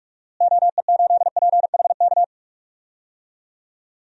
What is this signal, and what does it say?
Morse code, heard as a keyed tone, about 44 wpm, 700 Hz. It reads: OE8PHK
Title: DXCC-100 Call Signs - 6 Characters - Repeated 3x - 45wpm